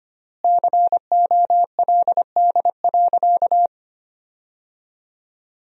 Morse code, heard as a keyed tone, about 25 wpm, 700 Hz. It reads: COLD.
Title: 4x Word-Spacing - Sentences from Top 400 Words - 25wpm